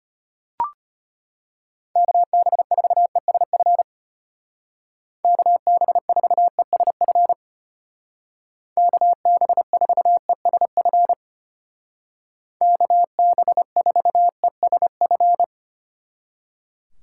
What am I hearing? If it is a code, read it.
Morse code, heard as a keyed tone, about 33 wpm, 700 Hz. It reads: KB4ESF KB4ESF KB4ESF KB4ESF